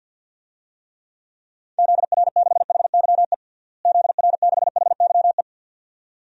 Morse code, heard as a keyed tone, about 53 wpm, 700 Hz. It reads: ZR6HYE ZR6HYE
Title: DXCC-100 Call Signs - 6 Characters - Repeated 2x - 50wpm